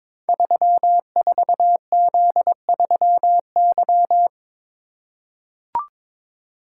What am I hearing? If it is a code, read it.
Morse code, heard as a keyed tone, about 22 wpm, 700 Hz. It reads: 34Z3Y
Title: Code Groups - 5 Letter:Numbers - 22wpm